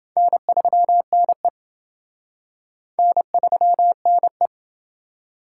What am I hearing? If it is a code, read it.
Morse code, heard as a keyed tone, about 28 wpm, 700 Hz. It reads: N3NE N3NE